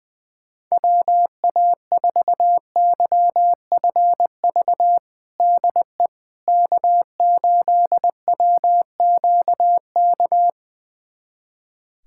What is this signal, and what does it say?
Morse code, heard as a keyed tone, about 21 wpm, 700 Hz. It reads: WA4YFV DE K8WQK
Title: USA Callsign DE Callsign 20wpm